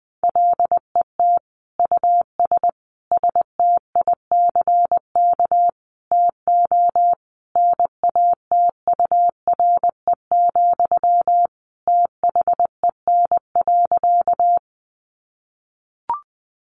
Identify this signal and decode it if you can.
Morse code, heard as a keyed tone, about 20 wpm, 700 Hz. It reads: LET US STICK TO NATURE, THEN.
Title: Sentences from Top 1200 Words 20wpm